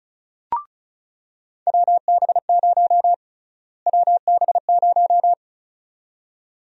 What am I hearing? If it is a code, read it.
Morse code, heard as a keyed tone, about 36 wpm, 700 Hz. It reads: WB0 WB0
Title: USA Call Sign Prefixes - Repeated 2x - 35wpm